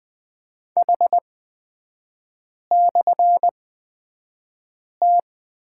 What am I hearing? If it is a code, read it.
Morse code, heard as a keyed tone, about 20 wpm, spaced as for 5 wpm, 700 Hz. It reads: H/T
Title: 236 Lesson 38e - Letter q - Three Characters 20wpm